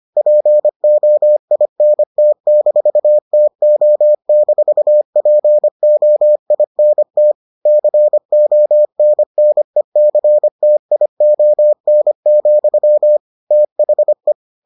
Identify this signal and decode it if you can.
Morse code, heard as a keyed tone, about 25 wpm, 600 Hz. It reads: POINT-TO-POINT CONNECTION, THE